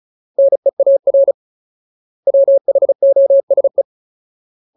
Morse code, heard as a keyed tone, about 35 wpm, 550 Hz. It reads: NEAR WHOSE
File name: Ultimate Word List / 35WPM Ultimate